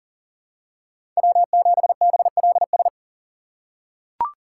Morse code, heard as a keyed tone, about 40 wpm, 700 Hz. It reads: W7BLS
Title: USA Call Signs - 5 Characters - 3x Character Spacing - 40wpm